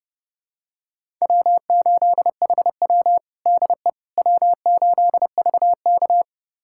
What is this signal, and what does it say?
Morse code, heard as a keyed tone, about 31 wpm, 700 Hz. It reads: W8HW DE W8VK
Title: USA Callsign DE Callsign - 4 Characters - 30wpm